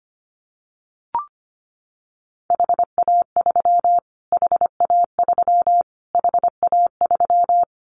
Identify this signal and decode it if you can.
Morse code, heard as a keyed tone, about 25 wpm, 700 Hz. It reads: HA3 HA3 HA3
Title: Callsign Prefixes for Top 25 DXCC - Repeated 3x - 25wpm